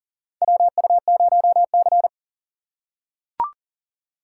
Morse code, heard as a keyed tone, about 40 wpm, 700 Hz. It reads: WU0C
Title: USA Call Signs - 3x Character Spacing - 40wpm